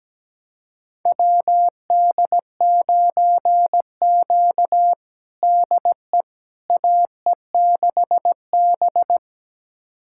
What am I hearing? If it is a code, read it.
Morse code, heard as a keyed tone, about 17 wpm, 700 Hz. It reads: WD9Q DE AE6B
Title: USA Callsign DE Callsign - 4 Characters - 17wpm